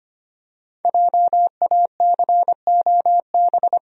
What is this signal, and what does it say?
Morse code, heard as a keyed tone, about 25 wpm, 700 Hz. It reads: JACOB
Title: USA Common Names - Max 5 Letters - 25wpm